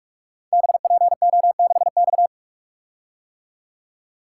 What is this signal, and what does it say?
Morse code, heard as a keyed tone, about 44 wpm, 700 Hz. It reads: BPO6X
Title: Code Groups - 5 Letter:Numbers - 45wpm